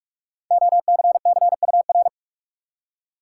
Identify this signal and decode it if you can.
Morse code, heard as a keyed tone, about 46 wpm, 700 Hz. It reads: OCCUR
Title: Five Letter Words - 45wpm